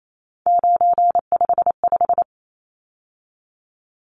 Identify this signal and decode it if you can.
Morse code, heard as a keyed tone, about 28 wpm, 700 Hz. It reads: 955
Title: Numbers - 3-Digit 28wpm